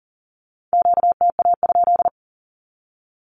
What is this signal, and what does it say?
Morse code, heard as a keyed tone, about 40 wpm, 700 Hz. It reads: QTA?
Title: Q-Codes 40wpm